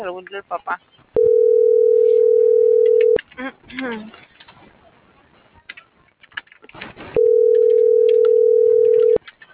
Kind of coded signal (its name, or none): telephone call progress tone